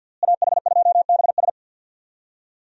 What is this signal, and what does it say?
Morse code, heard as a keyed tone, about 47 wpm, 700 Hz. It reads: AH2BS
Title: USA Call Signs - 5 Characters - Repeated 3x - 50wpm